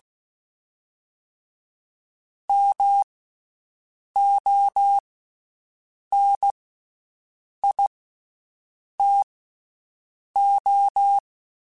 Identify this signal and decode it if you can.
Morse code, heard as a keyed tone, about 16 wpm, spaced as for 6 wpm, 800 Hz. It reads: MONITO
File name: Special INGs / Special INGs-mp3s / VerySlowToFaster Ing @15